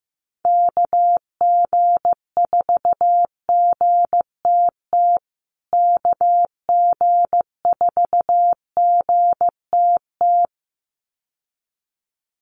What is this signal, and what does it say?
Morse code, heard as a keyed tone, about 14 wpm, 700 Hz. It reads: KG4GTT KG4GTT
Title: USA Call Signs Repeated 2x 15wpm